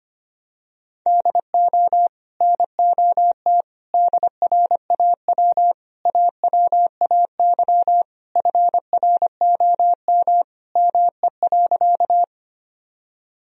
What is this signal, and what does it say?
Morse code, heard as a keyed tone, about 25 wpm, 700 Hz. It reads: DO NOT DRAW AWAY FROM ME.
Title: Sentences from Top 1000 Words - Review Part 1 - 25wpm